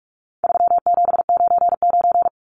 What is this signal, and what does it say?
Morse code, heard as a keyed tone, about 44 wpm, 700 Hz. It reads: 3799
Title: Numbers - 4-Digit 45wpm